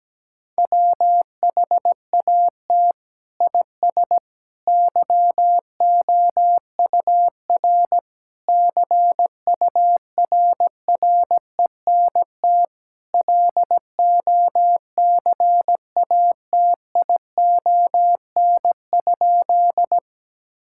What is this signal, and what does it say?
Morse code, heard as a keyed tone, about 17 wpm, 700 Hz. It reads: WHAT IS YOUR CURRENT LOCATION?
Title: Sentences from Top 1000 Words - Review Part 6 - 17wpm